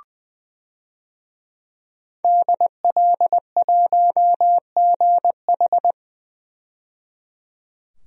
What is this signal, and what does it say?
Morse code, heard as a keyed tone, about 21 wpm, 700 Hz. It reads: DL1GH
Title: DXCC-100 Call Signs - 5 Characters - 20wpm